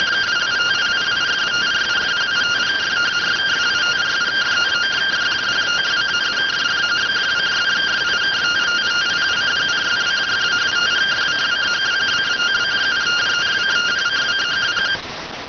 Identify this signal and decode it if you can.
Two-tone teleprinter signal, about 170 Hz shift.